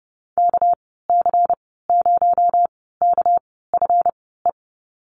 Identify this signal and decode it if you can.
Morse code, heard as a keyed tone, about 29 wpm, spaced as for 17 wpm, 700 Hz. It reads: KC0KFE
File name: Lesson 37g - Letter x - Call Signs 30wpm